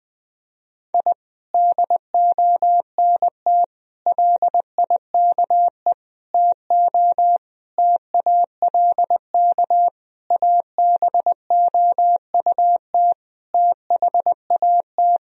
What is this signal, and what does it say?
Morse code, heard as a keyed tone, about 21 wpm, 700 Hz. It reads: I DONT LIKE TO TALK ABOUT THAT
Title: Sentences from Top 1000 Words - Review Part 1 - 20wpm